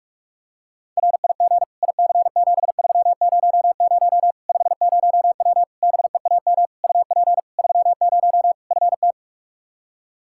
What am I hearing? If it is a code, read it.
Morse code, heard as a keyed tone, about 46 wpm, 700 Hz. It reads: RIG IC7300 50W BEAM UP 30 FT
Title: Practice QSO - Exchange 2 - Info - 45wpm